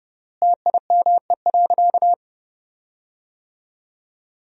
Morse code, heard as a keyed tone, about 30 wpm, 700 Hz. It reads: TIME.
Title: 3x Word-Spacing - Sentences from Top 100 Words - 30wpm